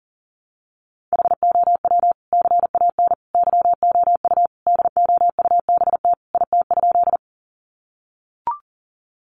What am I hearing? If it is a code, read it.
Morse code, heard as a keyed tone, about 40 wpm, 700 Hz. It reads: HOW CAN YOU DOUBT IT?
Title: Sentences from Top 1500 Words 40wpm